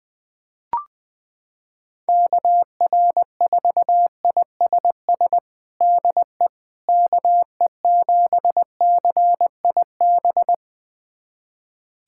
Morse code, heard as a keyed tone, about 21 wpm, 700 Hz. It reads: KR4ISS DE KE7CIB
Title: USA Callsign DE Callsign - 6 Characters - 20wpm